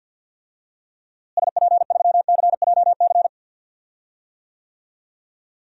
Morse code, heard as a keyed tone, about 47 wpm, 700 Hz. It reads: SP3CJC
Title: DXCC-100 Call Signs - 6 Characters - Repeated 3x - 50wpm